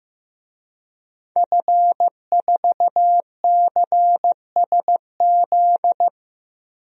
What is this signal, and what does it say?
Morse code, heard as a keyed tone, about 15 wpm, 700 Hz. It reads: F4CSZ
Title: DXCC-100 Call Signs - 5 Characters - 15wpm